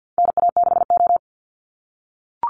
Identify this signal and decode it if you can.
Morse code, heard as a keyed tone, about 49 wpm, 700 Hz. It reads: NA6O
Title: NA SOTA Chaser Callsigns 50wpm